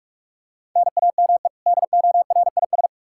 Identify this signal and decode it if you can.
Morse code, heard as a keyed tone, about 44 wpm, 700 Hz. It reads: NAME DORIS
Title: Practice QSO - Exchange 1 - Beginning - 45wpm